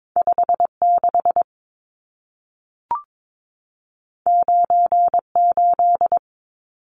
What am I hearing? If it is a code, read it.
Morse code, heard as a keyed tone, about 22 wpm, 700 Hz. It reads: 56 98
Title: Numbers - 2-Digit 22wpm